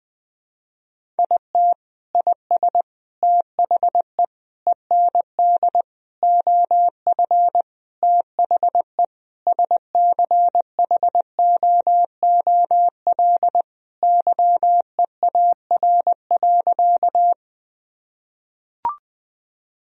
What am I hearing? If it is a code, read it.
Morse code, heard as a keyed tone, about 20 wpm, 700 Hz. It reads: IT IS THE END OF THE SCHOOL YEAR.
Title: Sentences from Top 1000 Words - Review Part 1 - 20wpm